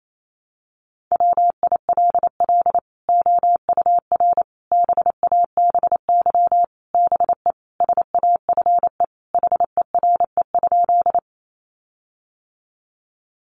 Morse code, heard as a keyed tone, about 28 wpm, 700 Hz. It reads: WILL OUR BABY BE SAFE HERE?
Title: Sentences from Top 1300 Words 28wpm